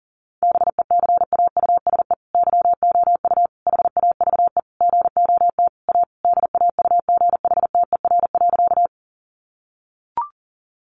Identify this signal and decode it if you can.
Morse code, heard as a keyed tone, about 40 wpm, 700 Hz. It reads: BECAUSE YOU HAVE GOT A DAUGHTER.
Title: Sentences from Top 1500 Words 40wpm